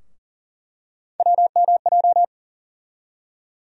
Morse code, heard as a keyed tone, about 40 wpm, 700 Hz. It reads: WMJ
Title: USA Call Sign Postfixes - 3 Letters 40wpm